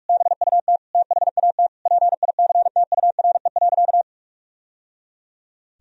Morse code, heard as a keyed tone, about 46 wpm, 700 Hz. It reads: BUT THAT PICTURE.
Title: Sentences from Top 1000 Words - Review Part 1 - 45wpm